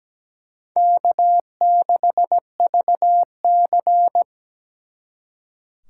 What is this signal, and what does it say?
Morse code, heard as a keyed tone, about 20 wpm, 700 Hz. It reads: K6VC
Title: Speed Racing - USA 1x2 Call Signs 17wpm